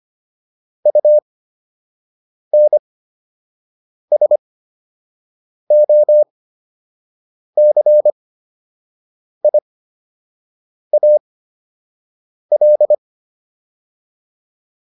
Morse code, heard as a keyed tone, about 25 wpm, spaced as for 6 wpm, 600 Hz. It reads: UNSOCIAL